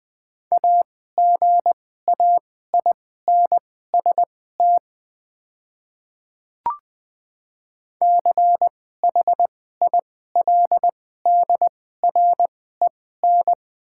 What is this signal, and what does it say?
Morse code, heard as a keyed tone, about 20 wpm, spaced as for 14 wpm, 690 Hz. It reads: AGAINST CHILDREN
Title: 2x Character Spacing - Top 500 Words - 20wpm